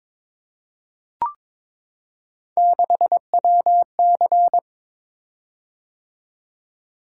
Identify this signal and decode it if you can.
Morse code, heard as a keyed tone, about 23 wpm, 700 Hz. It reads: 6WC
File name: practice / 22wpm / Three Letter-Number Combinations - 22wpm